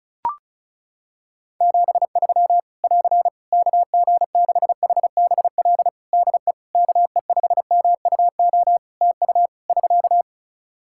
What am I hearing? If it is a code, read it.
Morse code, heard as a keyed tone, about 36 wpm, 700 Hz. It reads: 73 <AR> KG6HBL DE KE5MUO TU <SK>
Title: Practice QSO - Exchange 4 - Ending - 35wpm